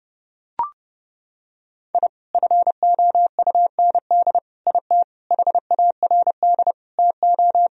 Morse code, heard as a keyed tone, about 30 wpm, 700 Hz. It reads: I FOUND IT HARD TO
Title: Sentences from Top 1000 Words - Review Part 1 - 30wpm